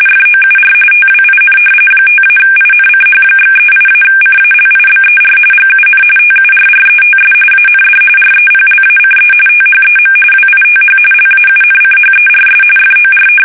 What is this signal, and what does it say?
RTTY, about 45.45 baud. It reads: FOUR TWIN 5-INCH/38-CALIBER GUN TURRETS AND REPLACE THEM WITH LAUNCHERS FOR
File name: fsk850 / battleshipnj